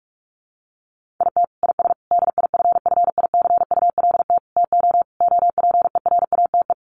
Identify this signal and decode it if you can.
Morse code, heard as a keyed tone, about 44 wpm, 700 Hz. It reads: IT IS DIFFICULT TO OPERATE
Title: Sentences from Top 1500 Words 45wpm